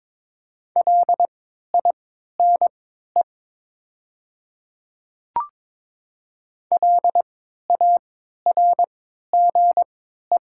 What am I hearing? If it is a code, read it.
Morse code, heard as a keyed tone, about 22 wpm, spaced as for 12 wpm, 700 Hz. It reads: LINE LARGE
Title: 3x Character Spacing - Top 300 Words - 22wpm